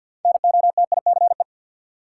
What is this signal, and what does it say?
Morse code, heard as a keyed tone, about 52 wpm, 700 Hz. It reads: NOTICE
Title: Sentences from Top 400 Words - New Word Drill - 50wpm